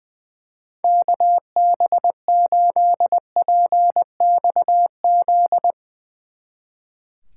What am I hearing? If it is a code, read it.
Morse code, heard as a keyed tone, about 22 wpm, 700 Hz. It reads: KB8PXZ
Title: Speed Racing - USA 2x3 Call Signs 20wpm